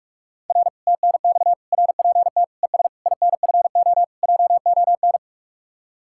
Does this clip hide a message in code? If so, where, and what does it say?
Yes: Morse code from 0.5 to 5.2 s, heard as a keyed tone, about 46 wpm, 700 Hz. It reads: R TNX RPT ES INFO JON